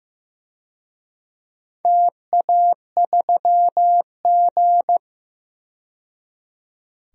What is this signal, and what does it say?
Morse code, heard as a keyed tone, about 15 wpm, 700 Hz. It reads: TA3G
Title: DXCC-100 Call Signs - 4 Characters - 15wpm